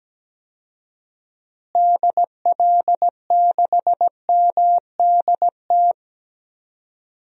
Morse code, heard as a keyed tone, about 17 wpm, 700 Hz. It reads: DL6MDT
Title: DXCC-100 Call Signs - 6 Characters - Repeated 2x - 17wpm